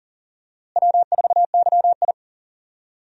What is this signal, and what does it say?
Morse code, heard as a keyed tone, about 40 wpm, 700 Hz. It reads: W4YI
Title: USA Call Signs - 4 Characters - Repeated 2x - 40wpm